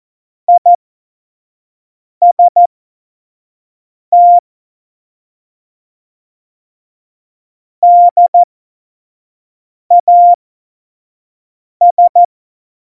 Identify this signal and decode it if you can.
Morse code, heard as a keyed tone, about 14 wpm, spaced as for 5 wpm, 700 Hz. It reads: IST DAS
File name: Morsekurs / Witze / Witz 11. Februar 2020, 25 F70